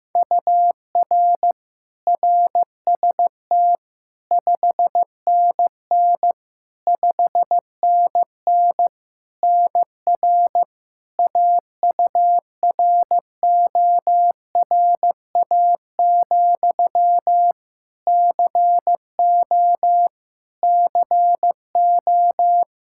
Morse code, heard as a keyed tone, about 16 wpm, 700 Hz. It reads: UR RST 5NN 5NN NR AURORA, CO CO
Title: Practice QSO - Exchange 1 - Beginning - 15wpm